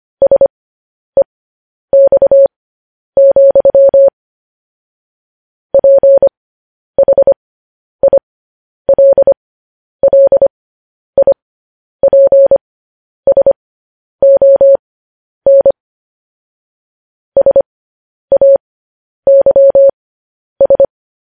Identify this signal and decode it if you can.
Morse code, heard as a keyed tone, about 25 wpm, spaced as for 10 wpm, 550 Hz. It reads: SEX, PHILLIPSON SAYS
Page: News Headlines in Morse Code at 10 WPM